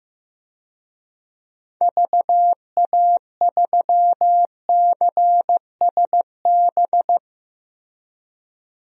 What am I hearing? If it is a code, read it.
Morse code, heard as a keyed tone, about 15 wpm, 700 Hz. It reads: VA3CSB
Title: DXCC-100 Call Signs - 6 Characters - Repeated 3x - 15wpm